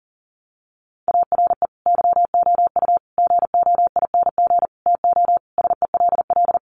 Morse code, heard as a keyed tone, about 40 wpm, 700 Hz. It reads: ARE YOU GOING TO SELL
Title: Sentences from Top 1200 Words 40wpm